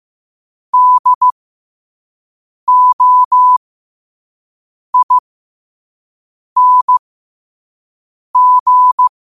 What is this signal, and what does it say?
Morse code, heard as a keyed tone, about 15 wpm, spaced as for 5 wpm, 1000 Hz. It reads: DOING
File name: qotdfiles / 5wpm